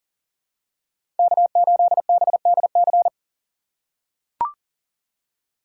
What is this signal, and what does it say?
Morse code, heard as a keyed tone, about 40 wpm, 700 Hz. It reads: K8BDC
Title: USA Call Signs - 5 Characters - 4x Character Spacing - 40wpm